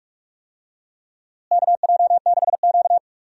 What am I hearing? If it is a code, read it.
Morse code, heard as a keyed tone, about 46 wpm, 700 Hz. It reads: KJ6Q
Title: USA Call Signs - 4 Characters - Repeated 3x - 45wpm